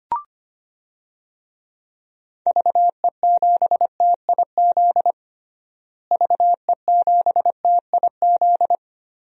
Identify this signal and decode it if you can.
Morse code, heard as a keyed tone, about 25 wpm, 700 Hz. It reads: VE7TIZ VE7TIZ
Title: DXCC-100 Call Signs - 6 Characters - Repeated 2x - 25wpm